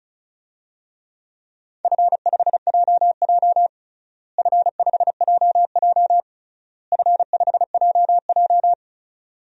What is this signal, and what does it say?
Morse code, heard as a keyed tone, about 36 wpm, 700 Hz. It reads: F5JJ F5JJ F5JJ
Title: DXCC-100 Call Signs - 4 Characters - Repeated 3x - 35wpm